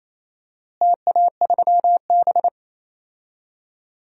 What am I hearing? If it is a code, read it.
Morse code, heard as a keyed tone, about 28 wpm, 700 Hz. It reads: TA3B
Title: DXCC-100 Call Signs - 4 Characters - Repeated 3x - 28wpm